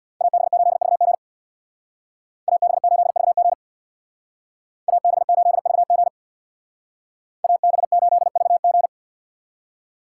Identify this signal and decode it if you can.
Morse code, heard as a keyed tone, about 64 wpm, 700 Hz. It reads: AB8VG AB8VG AB8VG AB8VG